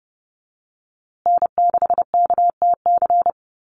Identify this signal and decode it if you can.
Morse code, heard as a keyed tone, about 30 wpm, 700 Hz. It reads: N6KTC
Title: USA Call Signs Repeated 3x 30wpm